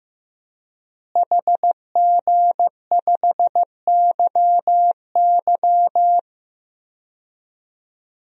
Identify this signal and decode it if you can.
Morse code, heard as a keyed tone, about 15 wpm, 700 Hz. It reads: HG5YY
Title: DXCC-100 Call Signs - 5 Characters - Repeated 3x - 15wpm